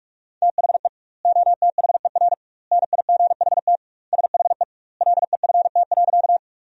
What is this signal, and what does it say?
Morse code, heard as a keyed tone, about 43 wpm, 700 Hz. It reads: THE OTHER NIGHT SHE LEFT.
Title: 2x Word-Spacing - Sentences from Top 300 Words - 45wpm